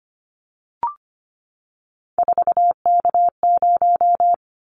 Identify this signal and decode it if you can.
Morse code, heard as a keyed tone, about 25 wpm, 700 Hz. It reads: 4K0
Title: Callsign Prefixes for Top 100 DXCC - 25wpm